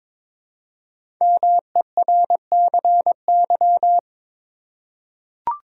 Morse code, heard as a keyed tone, about 23 wpm, 700 Hz. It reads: MERCY
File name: 5 Letter Words - First 3 Letters Given - 22wpm